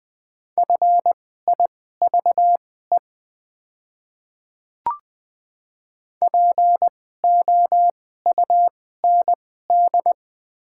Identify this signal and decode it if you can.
Morse code, heard as a keyed tone, about 20 wpm, spaced as for 14 wpm, 700 Hz. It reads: FIVE POUND
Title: 2x Character Spacing - Top 500 Words - 20wpm